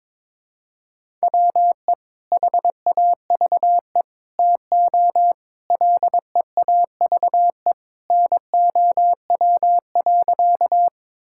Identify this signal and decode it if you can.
Morse code, heard as a keyed tone, about 22 wpm, 700 Hz. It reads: WE HAVE TO LEAVE NOW.
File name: practice / 22wpm / Sentences with 3-Word Phrases - 22wpm